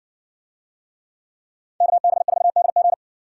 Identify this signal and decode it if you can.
Morse code, heard as a keyed tone, about 57 wpm, 700 Hz. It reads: KB4DG